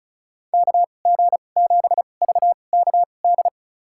Morse code, heard as a keyed tone, about 36 wpm, spaced as for 25 wpm, 700 Hz. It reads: KG7VKD
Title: USA Call Signs - 2x Character Spacing - 35wpm